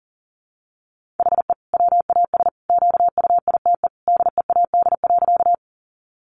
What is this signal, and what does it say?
Morse code, heard as a keyed tone, about 40 wpm, 700 Hz. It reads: HE WAS QUITE DEAD.
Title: Sentences from Top 1500 Words 40wpm